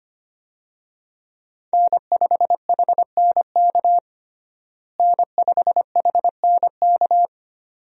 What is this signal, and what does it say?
Morse code, heard as a keyed tone, about 25 wpm, 700 Hz. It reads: N5HNK N5HNK
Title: USA Call Signs - 5 Characters - Repeated 2x - 25wpm